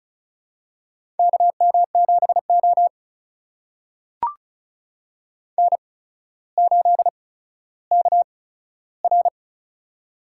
Morse code, heard as a keyed tone, about 34 wpm, spaced as for 15 wpm, 700 Hz. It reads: KM7O N8KR
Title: USA Call Signs - 4 Characters - 8x Character Spacing - 35wpm